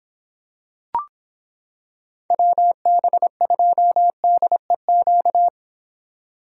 Morse code, heard as a keyed tone, about 26 wpm, 700 Hz. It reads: WB2DEQ